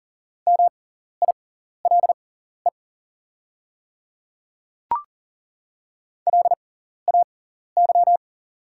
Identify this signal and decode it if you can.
Morse code, heard as a keyed tone, about 40 wpm, spaced as for 14 wpm, 700 Hz. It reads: MILE LAY
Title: 6x Character Spacing - Top 500 Words - 40wpm